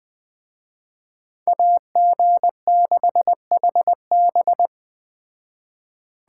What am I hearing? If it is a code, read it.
Morse code, heard as a keyed tone, about 20 wpm, 700 Hz. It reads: AG6HB
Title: Speed Racing - USA 2x2 Call Signs 17wpm